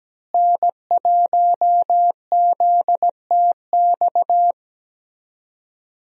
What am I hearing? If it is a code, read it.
Morse code, heard as a keyed tone, about 17 wpm, 700 Hz. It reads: N1ZTX
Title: USA Call Signs - 5 Characters - Repeated 3x - 17wpm